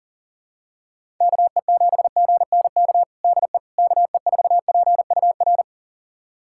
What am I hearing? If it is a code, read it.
Morse code, heard as a keyed tone, about 37 wpm, 700 Hz. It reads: KE7GNK DE KE4PUR
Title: USA Callsign DE Callsign - 6 Characters - 40wpm